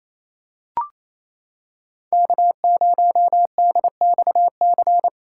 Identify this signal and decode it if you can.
Morse code, heard as a keyed tone, about 28 wpm, 700 Hz. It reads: K0DXC
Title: K1USN SST QSO Practice - 28wpm